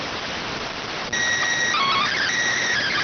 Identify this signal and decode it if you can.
SSTV (slow-scan television), PD 290.